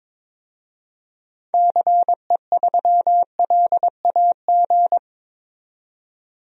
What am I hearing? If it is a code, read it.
Morse code, heard as a keyed tone, about 22 wpm, 700 Hz. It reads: CE3LAG
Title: DXCC-100 Call Signs - 6 Characters - 22wpm